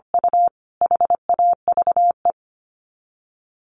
Morse code, heard as a keyed tone, about 25 wpm, 700 Hz. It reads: U HAVE
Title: Sets of 3 Words 25wpm